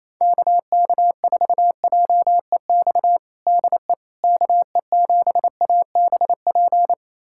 Morse code, heard as a keyed tone, about 27 wpm, 700 Hz. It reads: KK4JEX DE KE7ABP
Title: USA Callsign DE Callsign - 6 Characters - 28wpm